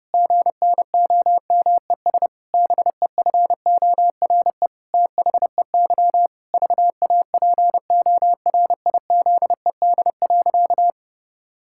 Morse code, heard as a keyed tone, about 32 wpm, 700 Hz. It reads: GNOMES BEFORE THEY VAPORIZED.
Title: Pangram Challenge v2 - 30wpm